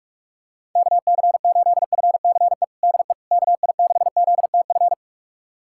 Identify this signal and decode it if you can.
Morse code, heard as a keyed tone, about 42 wpm, 700 Hz. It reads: KC8FCE DE KI6ZTF
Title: USA Callsign DE Callsign - 6 Characters - 45wpm